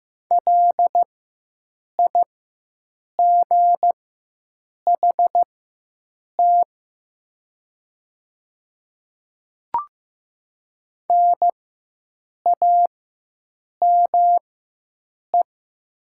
Morse code, heard as a keyed tone, about 15 wpm, spaced as for 7 wpm, 700 Hz. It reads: LIGHT NAME
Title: 4x Character Spacing - Top 200 Words - 15wpm